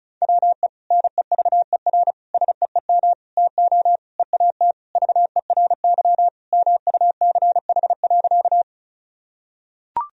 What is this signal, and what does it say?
Morse code, heard as a keyed tone, about 34 wpm, 700 Hz. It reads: WE NEVER SEEM TO EAT VERY MUCH.
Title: Sentences from Top 1000 Words - Review Part 1 - 35wpm